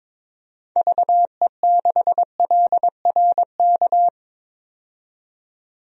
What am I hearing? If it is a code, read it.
Morse code, heard as a keyed tone, about 22 wpm, 700 Hz. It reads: VE6LRK
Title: DXCC-100 Call Signs - 6 Characters - 22wpm